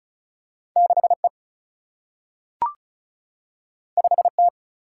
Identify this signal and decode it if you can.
Morse code, heard as a keyed tone, about 36 wpm, 700 Hz. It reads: 6 E 5 T
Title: Single-Letter Single-Number 35wpm